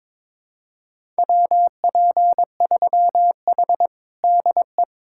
Morse code, heard as a keyed tone, about 22 wpm, 700 Hz. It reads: WP3H DE
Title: USA Callsign DE Callsign 22wpm